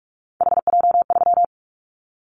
Morse code, heard as a keyed tone, about 46 wpm, 700 Hz. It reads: HJ3
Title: Callsign Prefixes for Top 100 DXCC - 45wpm